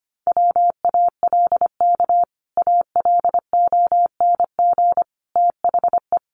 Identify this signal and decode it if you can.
Morse code, heard as a keyed tone, about 25 wpm, 700 Hz. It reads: WALK ALONG THE
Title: Sentences from Top 1100 Words 25wpm